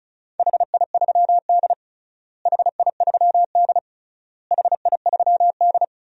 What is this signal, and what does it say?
Morse code, heard as a keyed tone, about 34 wpm, 700 Hz. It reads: HI3D HI3D HI3D
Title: DXCC-100 Call Signs - 4 Characters - Repeated 3x - 35wpm